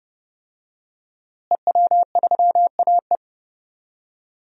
Morse code, heard as a keyed tone, about 30 wpm, 700 Hz. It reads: EW3AE
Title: DXCC-100 Call Signs - 5 Characters - Repeated 2x - 30wpm